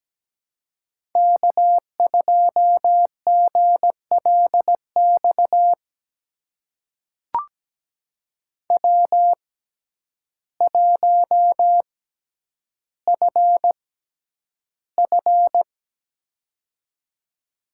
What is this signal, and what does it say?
Morse code, heard as a keyed tone, about 17 wpm, spaced as for 9 wpm, 700 Hz. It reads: K2GLX W1FF
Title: USA Call Signs - 6x Character Spacing - 17wpm